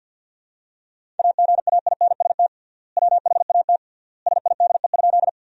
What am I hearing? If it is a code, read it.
Morse code, heard as a keyed tone, about 50 wpm, 700 Hz. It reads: AGAINST WHAT SIDE?
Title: 3x Word-Spacing - Sentences from Top 400 Words - 50wpm